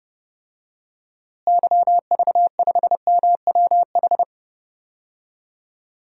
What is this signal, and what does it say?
Morse code, heard as a keyed tone, about 30 wpm, 700 Hz. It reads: YV5MWH
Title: DXCC-100 Call Signs - 6 Characters - Repeated 2x - 30wpm